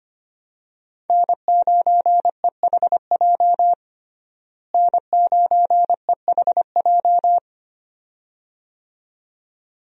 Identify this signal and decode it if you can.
Morse code, heard as a keyed tone, about 25 wpm, 700 Hz. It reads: N9EHJ N9EHJ
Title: USA Call Signs - 5 Characters - Repeated 2x - 25wpm